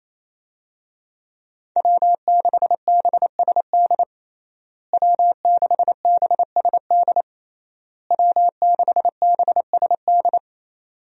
Morse code, heard as a keyed tone, about 28 wpm, 700 Hz. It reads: W6BSD W6BSD W6BSD
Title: USA Call Signs - 5 Characters - Repeated 3x - 28wpm